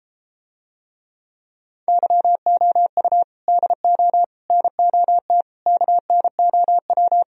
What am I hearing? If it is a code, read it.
Morse code, heard as a keyed tone, about 33 wpm, 700 Hz. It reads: YOU DO NOT KNOW